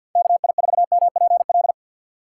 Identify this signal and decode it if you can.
Morse code, heard as a keyed tone, about 52 wpm, 700 Hz. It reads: KI4MPL
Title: USA Call Signs - 6 Characters - 6x Character Spacing - 50wpm